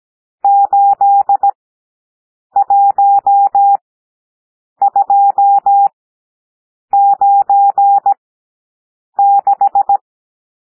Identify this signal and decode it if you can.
Morse code, heard as a keyed tone, about 17 wpm, spaced as for 7 wpm, 800 Hz. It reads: 81296
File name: SOUND NUMBERS 17 wpm